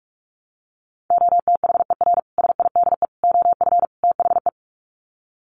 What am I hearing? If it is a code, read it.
Morse code, heard as a keyed tone, about 44 wpm, 700 Hz. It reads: OTHER SIDE OF THE
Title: Sets of 4 Words 45wpm